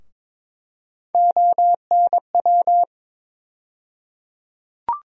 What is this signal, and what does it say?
Morse code, heard as a keyed tone, about 22 wpm, 700 Hz. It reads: ONW
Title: USA Call Sign Postfixes - 3 Letters 22wpm